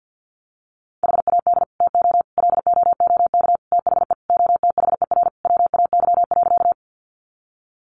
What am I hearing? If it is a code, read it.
Morse code, heard as a keyed tone, about 48 wpm, 700 Hz. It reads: HAD TO LOOK THE OTHER WAY.
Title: Sentences from Top 100 Words 50wpm